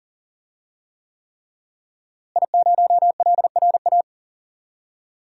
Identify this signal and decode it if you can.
Morse code, heard as a keyed tone, about 40 wpm, 700 Hz. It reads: I0LRA
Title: DXCC-100 Call Signs - 5 Characters - Repeated 3x - 40wpm